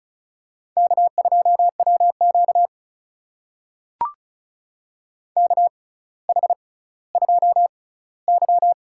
Morse code, heard as a keyed tone, about 36 wpm, spaced as for 18 wpm, 700 Hz. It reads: K2WQ KH2Y
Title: USA Call Signs - 4 Characters - 6x Character Spacing - 35wpm